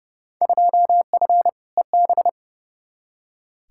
Morse code, heard as a keyed tone, about 30 wpm, 700 Hz. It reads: 2F EB
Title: Field Day Exchanges Encore - 30wpm